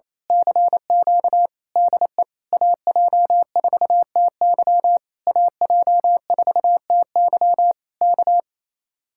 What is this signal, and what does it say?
Morse code, heard as a keyed tone, about 26 wpm, 700 Hz. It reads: CQ DE AJ4TY AJ4TY K
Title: Speed-Racing Common QSO Phrases Encore - 28wpm